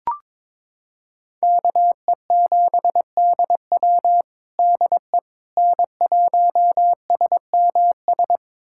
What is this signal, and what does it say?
Morse code, heard as a keyed tone, about 22 wpm, 700 Hz. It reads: KE7DW DE N1SMS
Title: USA Callsign DE Callsign - 5 Characters - 22wpm